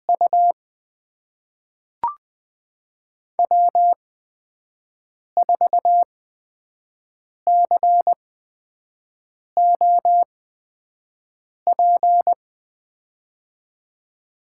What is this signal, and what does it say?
Morse code, heard as a keyed tone, about 20 wpm, spaced as for 5 wpm, 700 Hz. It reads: U W4COP
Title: USA Call Signs - 5 Characters - 8x Character Spacing - 20wpm